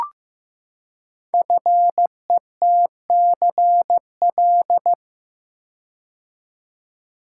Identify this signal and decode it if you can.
Morse code, heard as a keyed tone, about 15 wpm, 700 Hz. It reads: FETCL
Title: Code Groups - 5 Letters - 15wpm